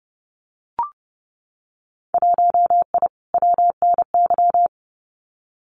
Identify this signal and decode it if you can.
Morse code, heard as a keyed tone, about 30 wpm, 700 Hz. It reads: 1I WNY
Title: Winter Field Day Exchanges Encore 30wpm